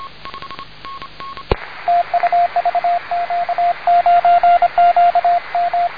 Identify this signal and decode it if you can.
Morse code, heard as a keyed tone, about 25 wpm, 700 Hz, with radio noise. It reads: TU VQ9QM
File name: om3cug / vq9qm10